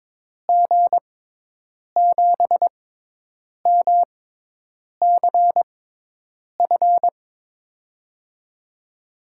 Morse code, heard as a keyed tone, about 22 wpm, spaced as for 8 wpm, 700 Hz. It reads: G7MCF
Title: USA Call Signs - 6 Characters - 6x Character Spacing - 22wpm